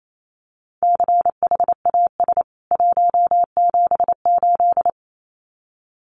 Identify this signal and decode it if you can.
Morse code, heard as a keyed tone, about 28 wpm, 700 Hz. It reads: CHAS 178
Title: CWT Exchange 28wpm